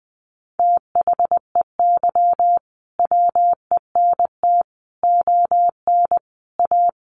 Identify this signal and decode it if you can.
Morse code, heard as a keyed tone, about 20 wpm, 700 Hz. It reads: THEY WENT ON A
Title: Sentences from Top 1400 Words 20wpm